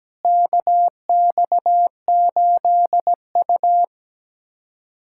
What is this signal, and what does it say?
Morse code, heard as a keyed tone, about 17 wpm, 700 Hz. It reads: KX8U
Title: USA Callsign DE Callsign - 4 Characters - 17wpm